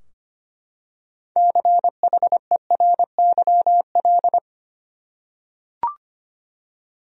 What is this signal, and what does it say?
Morse code, heard as a keyed tone, about 25 wpm, 700 Hz. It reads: CHERYL
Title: USA Common Names - 25wpm